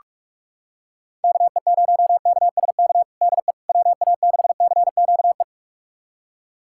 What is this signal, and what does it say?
Morse code, heard as a keyed tone, about 46 wpm, 700 Hz. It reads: KE0KSK DE WA6CQE
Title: USA Callsign DE Callsign 45wpm